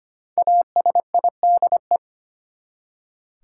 Morse code, heard as a keyed tone, about 25 wpm, 700 Hz. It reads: ASIDE
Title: Five Letter Words - 25wpm